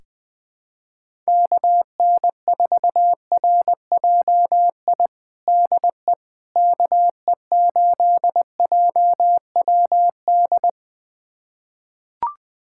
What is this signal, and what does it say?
Morse code, heard as a keyed tone, about 20 wpm, 700 Hz. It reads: KN4RJI DE KE8JWD
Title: USA Callsign DE Callsign - 6 Characters - 20wpm